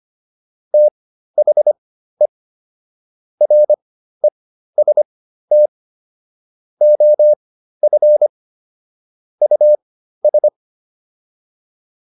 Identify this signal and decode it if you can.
Morse code, heard as a keyed tone, about 25 wpm, spaced as for 13 wpm, 600 Hz. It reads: THE REST OF US